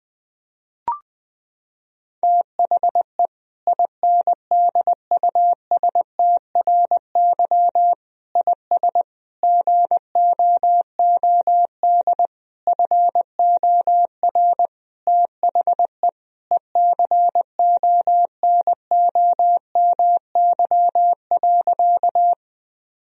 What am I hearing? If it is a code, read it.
Morse code, heard as a keyed tone, about 20 wpm, 700 Hz. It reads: THE INDUSTRY IS GOOD FOR THE ECONOMY.
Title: Sentences from Top 1000 Words - Review Part 4 - 20wpm